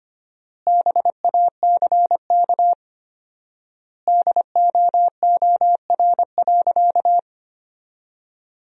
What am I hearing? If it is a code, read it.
Morse code, heard as a keyed tone, about 25 wpm, 700 Hz. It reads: BACK DOOR.
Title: 4x Word-Spacing - Sentences from Top 300 Words - 25wpm